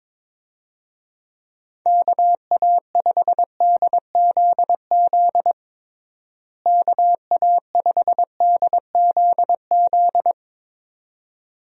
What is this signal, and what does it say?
Morse code, heard as a keyed tone, about 21 wpm, 700 Hz. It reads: KA5DZZ KA5DZZ
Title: USA Call Signs - 6 Characters - Repeated 2x - 22wpm